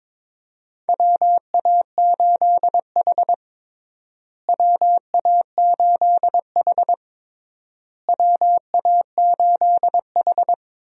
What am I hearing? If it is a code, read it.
Morse code, heard as a keyed tone, about 22 wpm, 700 Hz. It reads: WA8H WA8H WA8H
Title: USA Call Signs - 4 Characters - Repeated 3x - 22wpm